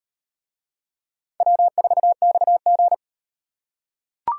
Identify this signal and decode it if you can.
Morse code, heard as a keyed tone, about 36 wpm, 700 Hz. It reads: W4XG